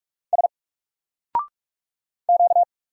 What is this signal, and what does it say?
Morse code, heard as a keyed tone, about 46 wpm, 700 Hz. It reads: S Q
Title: Single Character (Letter:Number:Punctuation:Prosign) 45wpm